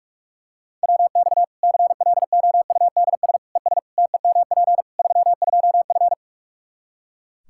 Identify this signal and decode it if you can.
Morse code, heard as a keyed tone, about 41 wpm, 700 Hz. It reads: WX CLOUDS ES TEMP 32F
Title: Practice QSO - Exchange 2 - Info - 45wpm